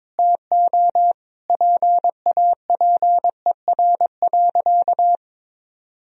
Morse code, heard as a keyed tone, about 22 wpm, 700 Hz. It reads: TO PAPER.
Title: Sentences from Top 1000 Words - Review Part 2 - 22wpm